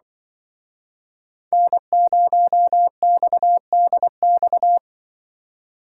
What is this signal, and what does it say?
Morse code, heard as a keyed tone, about 24 wpm, 700 Hz. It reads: N0XDX
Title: Speed Racing - USA Call Signs 20wpm